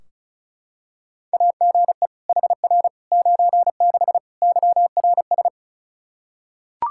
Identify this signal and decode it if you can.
Morse code, heard as a keyed tone, about 36 wpm, 700 Hz. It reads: AGE HR 96 YRS
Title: K1USN SST QSO Practice 35wpm